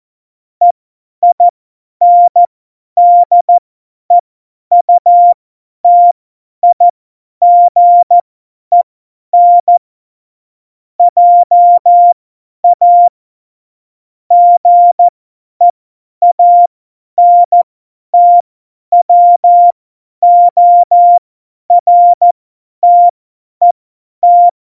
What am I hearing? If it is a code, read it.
Morse code, heard as a keyed tone, about 14 wpm, spaced as for 10 wpm, 700 Hz. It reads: EINDEUTIGEN JA GEANTWORTET
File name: Morsekurs / Witze / Witz 13. August 2019, 50 F70